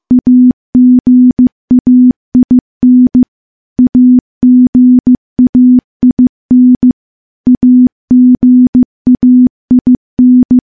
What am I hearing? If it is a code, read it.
Morse code, heard as a keyed tone, about 15 wpm, 250 Hz. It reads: AGAIN AGAIN AGAIN
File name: morse / morse-training-15WPM